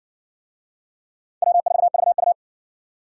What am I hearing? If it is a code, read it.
Morse code, heard as a keyed tone, about 58 wpm, 700 Hz. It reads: W4VU